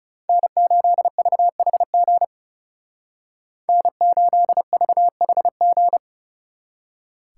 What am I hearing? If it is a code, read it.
Morse code, heard as a keyed tone, about 37 wpm, 700 Hz. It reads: N8VHG N8VHG
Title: Speed Racing - USA 1x3 Call Signs 30wpm